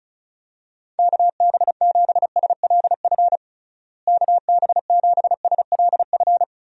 Morse code, heard as a keyed tone, about 34 wpm, 700 Hz. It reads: KB7SLF KB7SLF
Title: USA Call Signs - 6 Characters - Repeated 2x - 35wpm